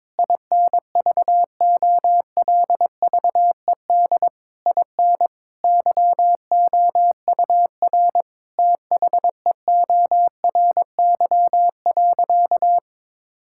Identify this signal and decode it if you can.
Morse code, heard as a keyed tone, about 22 wpm, 700 Hz. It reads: INVOLVED IN YOUR THEORY.
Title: Sentences from Top 1000 Words - Review Part 5 - 22wpm